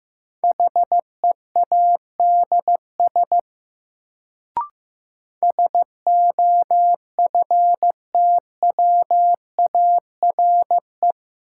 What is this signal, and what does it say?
Morse code, heard as a keyed tone, about 15 wpm, 700 Hz. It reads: HEADS SOFTWARE
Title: Sentences from Top 1000 Words - Review Part 3 - New Word Drill - 15wpm